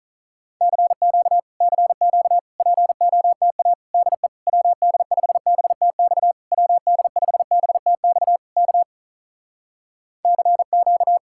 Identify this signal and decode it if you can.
Morse code, heard as a keyed tone, about 46 wpm, 700 Hz. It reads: CQ CQ POTA DE WD5BTX WD5BTX K CQ